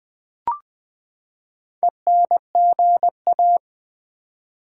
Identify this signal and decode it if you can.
Morse code, heard as a keyed tone, about 20 wpm, 700 Hz. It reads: ENGA
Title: Build A Word - 4 Letters At A Time - 10-Letter Words - 20wpm